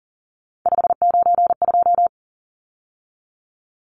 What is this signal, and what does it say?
Morse code, heard as a keyed tone, about 40 wpm, 700 Hz. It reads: 592
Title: Numbers - 3-Digit 40wpm